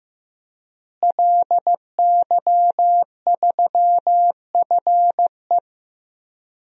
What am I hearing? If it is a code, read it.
Morse code, heard as a keyed tone, about 15 wpm, 700 Hz. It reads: LY3FE
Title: DXCC-100 Call Signs - 5 Characters - Repeated 3x - 15wpm